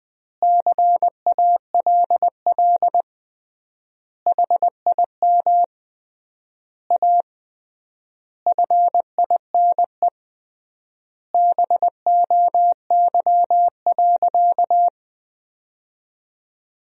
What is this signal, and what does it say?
Morse code, heard as a keyed tone, about 20 wpm, 700 Hz. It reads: CALL HIM A FINE BOY.
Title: 3x Word-Spacing - Sentences from Top 500 Words - 20wpm